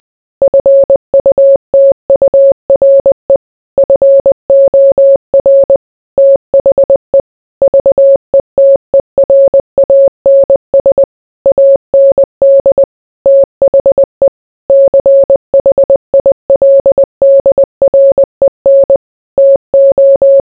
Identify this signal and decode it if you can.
Morse code, heard as a keyed tone, about 20 wpm, 550 Hz. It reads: FUTURE FOR THE VETERANS AND THE CHILDREN TO